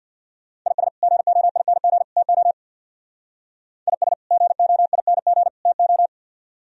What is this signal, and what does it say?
Morse code, heard as a keyed tone, about 60 wpm, 700 Hz. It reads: IS GOING TO IS GOING TO
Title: Speed-Racing - Sets of 3 Words 50wpm